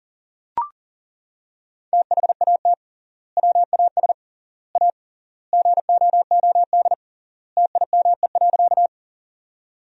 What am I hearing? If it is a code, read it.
Morse code, heard as a keyed tone, about 40 wpm, 700 Hz. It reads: THAT WAS A GOOD TIME.
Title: 3x Word-Spacing - Sentences from Top 200 Words - 40wpm